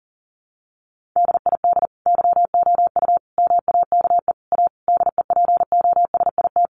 Morse code, heard as a keyed tone, about 40 wpm, 700 Hz. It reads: DID YOU MAKE A DEPOSIT
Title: Sentences from Top 1300 Words 40wpm